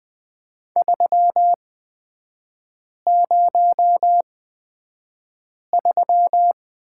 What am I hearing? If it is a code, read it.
Morse code, heard as a keyed tone, about 20 wpm, spaced as for 5 wpm, 700 Hz. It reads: 303
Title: 236 Lesson 38e - Letter q - Three Characters 20wpm